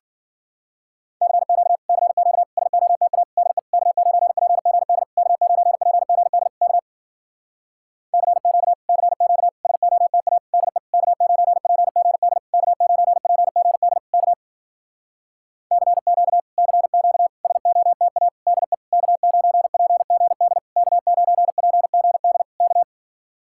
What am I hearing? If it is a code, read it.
Morse code, heard as a keyed tone, about 52 wpm, 700 Hz. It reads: CQ CQ SOTA DE K9PGD K9PGD K CQ CQ SOTA DE K9PGD K9PGD K CQ CQ SOTA DE K9PGD K9PGD K